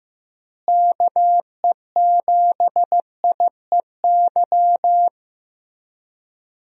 Morse code, heard as a keyed tone, about 15 wpm, 700 Hz. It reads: KE7IEY
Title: USA Call Signs - 6 Characters - Repeated 3x - 15wpm